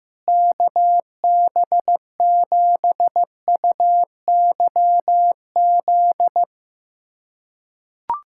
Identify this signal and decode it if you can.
Morse code, heard as a keyed tone, about 15 wpm, 700 Hz. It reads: KB7UYZ
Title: USA Callsign DE Callsign 15wpm